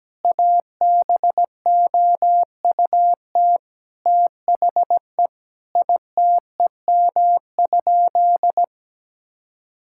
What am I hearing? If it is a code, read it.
Morse code, heard as a keyed tone, about 17 wpm, 700 Hz. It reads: ABOUT THE ITEM?
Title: Sentences from Top 1000 Words - Review Part 6 - 17wpm